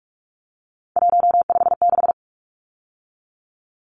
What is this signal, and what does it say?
Morse code, heard as a keyed tone, about 44 wpm, 700 Hz. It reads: 156
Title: Numbers - 3-Digit 45wpm